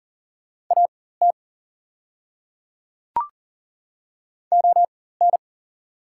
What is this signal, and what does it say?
Morse code, heard as a keyed tone, about 38 wpm, spaced as for 18 wpm, 700 Hz. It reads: AT ON
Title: 4x Character Spacing - Top 200 Words - 40wpm